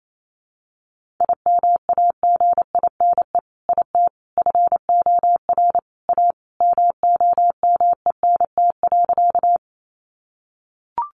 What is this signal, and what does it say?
Morse code, heard as a keyed tone, about 28 wpm, 700 Hz. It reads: IMAGINE IT FOR A MOMENT.
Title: Sentences from Top 1400 Words 28wpm